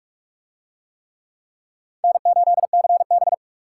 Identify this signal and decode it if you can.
Morse code, heard as a keyed tone, about 48 wpm, 700 Hz. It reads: N8CB